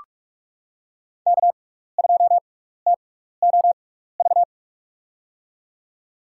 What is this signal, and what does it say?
Morse code, heard as a keyed tone, about 45 wpm, spaced as for 15 wpm, 700 Hz. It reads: K2TOV
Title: USA Call Signs - 5 Characters - 6x Character Spacing - 45wpm